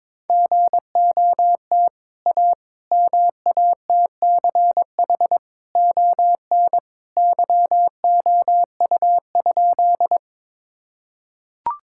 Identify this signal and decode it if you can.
Morse code, heard as a keyed tone, about 22 wpm, 700 Hz. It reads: GOT A MATCH ON YOU?
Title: Sentences from Top 1600 Words 22wpm